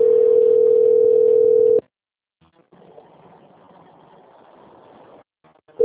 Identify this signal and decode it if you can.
Ringback tone (North America), 440 + 480 Hz.